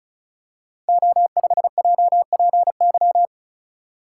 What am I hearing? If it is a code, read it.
Morse code, heard as a keyed tone, about 36 wpm, 700 Hz. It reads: O5JPY
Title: Code Groups - 5 Letter:Numbers - 35wpm